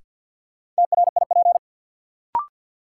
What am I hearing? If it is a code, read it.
Morse code, heard as a keyed tone, about 51 wpm, 700 Hz. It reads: TRIP
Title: Sentences from Top 1000 Words - Review Part 4 - New Word Drill - 50wpm